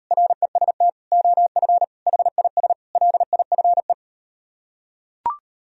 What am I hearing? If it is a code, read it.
Morse code, heard as a keyed tone, about 38 wpm, 700 Hz. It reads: REST OF HIS LIFE